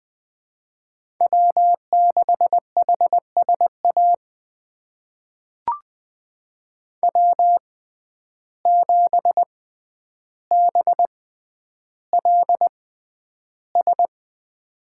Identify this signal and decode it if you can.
Morse code, heard as a keyed tone, about 20 wpm, spaced as for 10 wpm, 700 Hz. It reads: W6HSA W7BLS
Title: USA Call Signs - 5 Characters - 6x Character Spacing - 20wpm